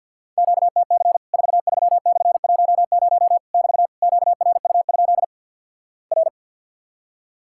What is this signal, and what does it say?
Morse code, heard as a keyed tone, about 53 wpm, 700 Hz. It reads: QTC 43/10 <BT> QRU? R
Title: Worked All Europe DX Contest - QTC Exchange with Repeats - 50wpm